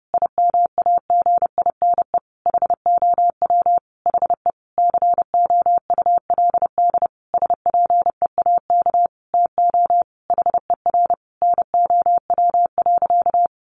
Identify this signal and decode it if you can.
Morse code, heard as a keyed tone, about 30 wpm, 700 Hz. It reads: IMAGINE HOW HE COULD SPEAK TO HER NOW.
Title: Sentences from Top 1400 Words 30wpm